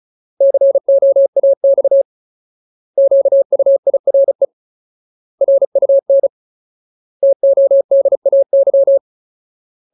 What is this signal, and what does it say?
Morse code, heard as a keyed tone, about 34 wpm, 550 Hz. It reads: COAX QUIRE RUN TODAY
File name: Ultimate Word List / 35WPM Ultimate